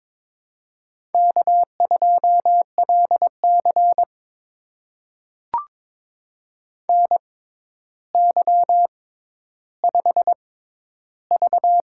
Morse code, heard as a keyed tone, about 22 wpm, spaced as for 11 wpm, 700 Hz. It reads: K2LC NY5V
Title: USA Call Signs - 4 Characters - 6x Character Spacing - 22wpm